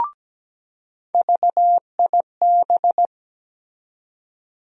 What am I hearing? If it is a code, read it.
Morse code, heard as a keyed tone, about 17 wpm, 700 Hz. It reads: VIB
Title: USA Call Sign Postfixes - 3 Letters 17wpm